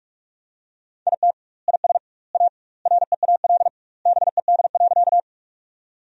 Morse code, heard as a keyed tone, about 46 wpm, 700 Hz. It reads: IT IS A REAL BED.
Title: 2x Word-Spacing - Sentences from Top 500 Words - 45wpm